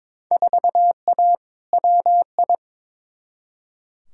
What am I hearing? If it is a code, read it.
Morse code, heard as a keyed tone, about 22 wpm, 700 Hz. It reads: 4A WI
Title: Field Day Exchanges - 22wpm